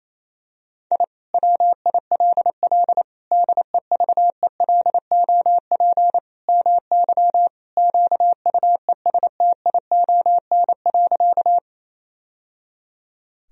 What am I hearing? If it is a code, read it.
Morse code, heard as a keyed tone, about 28 wpm, 700 Hz. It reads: I WILL DEVELOP MY QUESTION.
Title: Sentences from Top 1000 Words - Review Part 2 - 28wpm